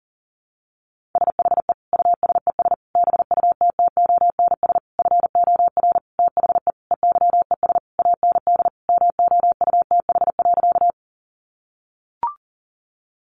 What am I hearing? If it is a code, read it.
Morse code, heard as a keyed tone, about 40 wpm, 700 Hz. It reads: SHE USES BUTTONS FOR THE EYES AND MOUTH.
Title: Easy Phrases 40wpm